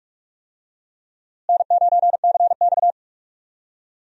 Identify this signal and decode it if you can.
Morse code, heard as a keyed tone, about 46 wpm, 700 Hz. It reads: N9CX
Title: USA Call Signs - 4 Characters - Repeated 2x - 45wpm